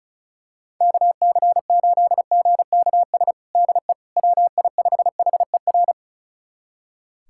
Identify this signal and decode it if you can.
Morse code, heard as a keyed tone, about 36 wpm, 700 Hz. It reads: KC8GKS DE WI5HER
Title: USA Callsign DE Callsign 35wpm